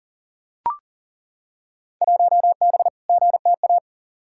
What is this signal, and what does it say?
Morse code, heard as a keyed tone, about 40 wpm, 700 Hz. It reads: 1B GTA
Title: Field Day Exchanges Encore - 40wpm